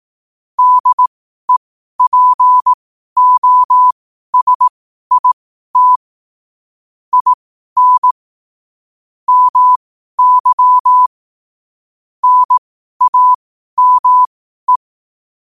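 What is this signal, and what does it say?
Morse code, heard as a keyed tone, about 18 wpm, spaced as for 12 wpm, 1000 Hz. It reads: DEPOSIT IN MY NAME
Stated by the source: Morse code sent at 12 words per minute.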